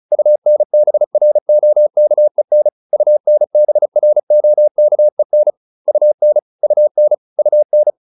Morse code, heard as a keyed tone, about 34 wpm, 600 Hz. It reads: UNBROKEN UNBROKEN UN UN UN
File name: UN8 35